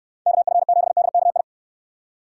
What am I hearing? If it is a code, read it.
Morse code, heard as a keyed tone, about 69 wpm, 700 Hz. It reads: KF7DKI